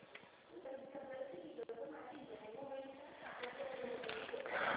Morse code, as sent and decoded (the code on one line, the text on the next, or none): none